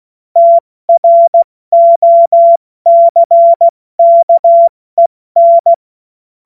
Morse code, heard as a keyed tone, about 16 wpm, spaced as for 14 wpm, 680 Hz. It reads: TROCKEN
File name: Morsekurs / Witze / Witz 21.November 2023 70 F80